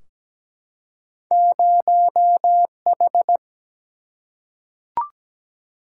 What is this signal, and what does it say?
Morse code, heard as a keyed tone, about 17 wpm, 700 Hz. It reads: 0H
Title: Single-Letter Single-Number 17wpm